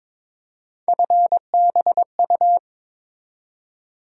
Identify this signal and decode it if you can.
Morse code, heard as a keyed tone, about 22 wpm, 700 Hz. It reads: FBU
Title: USA Call Sign Postfixes - 3 Letters 22wpm